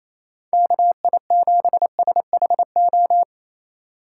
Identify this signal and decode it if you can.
Morse code, heard as a keyed tone, about 28 wpm, 700 Hz. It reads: KI7SHO
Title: USA Call Signs - 6 Characters - 4x Character Spacing - 28wpm